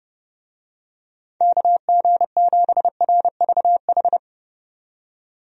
Morse code, heard as a keyed tone, about 30 wpm, 700 Hz. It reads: KG7RVH
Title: USA Call Signs - 6 Characters - 30wpm